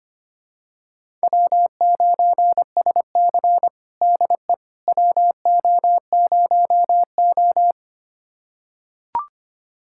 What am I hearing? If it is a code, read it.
Morse code, heard as a keyed tone, about 25 wpm, 700 Hz. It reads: W9SC DE WO0O
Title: USA Callsign DE Callsign - 4 Characters - 25wpm